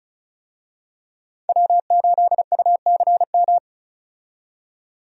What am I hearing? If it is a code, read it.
Morse code, heard as a keyed tone, about 34 wpm, 700 Hz. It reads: W8UCM
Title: USA Call Signs - 5 Characters - 35wpm